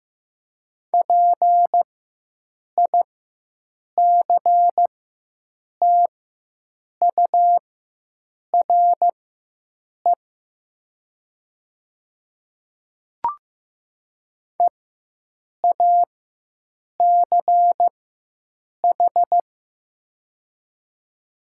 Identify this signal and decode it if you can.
Morse code, heard as a keyed tone, about 15 wpm, spaced as for 7 wpm, 700 Hz. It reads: PICTURE EACH
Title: 4x Character Spacing - Top 300 Words - 15wpm